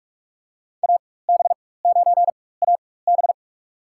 Morse code, heard as a keyed tone, about 44 wpm, spaced as for 21 wpm, 700 Hz. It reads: AB9AB
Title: USA Call Signs - 5 Characters - 4x Character Spacing - 45wpm